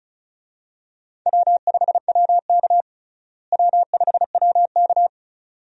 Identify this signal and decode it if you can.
Morse code, heard as a keyed tone, about 34 wpm, 700 Hz. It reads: W5WK W5WK
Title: USA Call Signs - 4 Characters - Repeated 2x - 35wpm